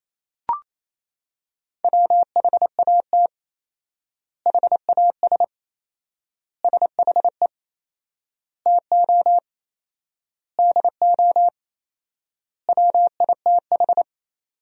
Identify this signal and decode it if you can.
Morse code, heard as a keyed tone, about 28 wpm, 700 Hz. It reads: WHAT HAS SHE TO DO WITH
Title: 4x Word-Spacing - Sentences from Top 100 Words - 28wpm